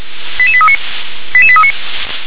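Five-tone selective call (ZVEI), heard as quick runs of five tones, tones of about 70 ms. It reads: 88839 88839
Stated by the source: ZVEI five-tone selective call received on an SDR.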